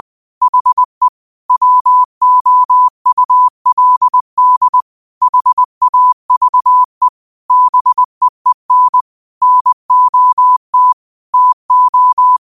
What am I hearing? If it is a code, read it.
Morse code, heard as a keyed tone, about 20 wpm, 1000 Hz. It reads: HE WOULD HAVE BEEN NOT TO
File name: qotdfiles / 20wpm